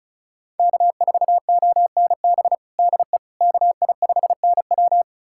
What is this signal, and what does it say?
Morse code, heard as a keyed tone, about 36 wpm, 700 Hz. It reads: K4ONB DE KI5NW
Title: USA Callsign DE Callsign 35wpm